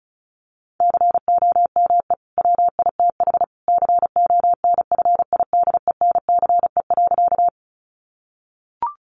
Morse code, heard as a keyed tone, about 34 wpm, 700 Hz. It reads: COME WITH CONFIDENCE.
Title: Sentences from Top 1500 Words 35wpm